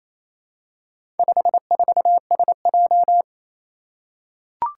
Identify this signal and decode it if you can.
Morse code, heard as a keyed tone, about 27 wpm, 700 Hz. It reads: 54SJ
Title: Four Challenging Letter-Number Combinations 28wpm